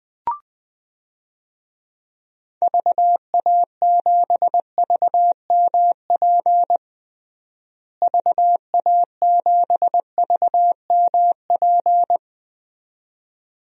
Morse code, heard as a keyed tone, about 21 wpm, 700 Hz. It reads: VA7VMP VA7VMP
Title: DXCC-100 Call Signs - 6 Characters - Repeated 2x - 20wpm